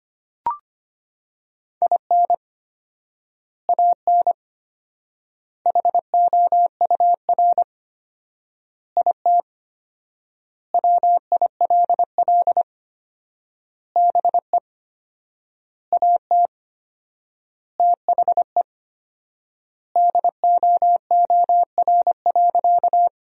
Morse code, heard as a keyed tone, about 25 wpm, 700 Hz. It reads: IN AN HOUR IT WILL BE AT THE DOOR.
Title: 4x Word-Spacing - Sentences from Top 400 Words - 25wpm